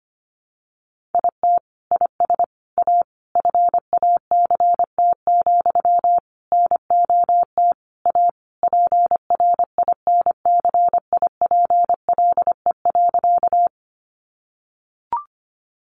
Morse code, heard as a keyed tone, about 25 wpm, 700 Hz. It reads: IT IS A FACT, NOT A PRINCIPLE.
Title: Sentences from Top 1400 Words 25wpm